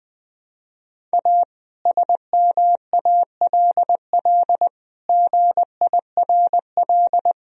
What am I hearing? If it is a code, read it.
Morse code, heard as a keyed tone, about 20 wpm, 700 Hz. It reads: A SMALL GIRL
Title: Sentences from Top 1000 Words - Review Part 1 - 20wpm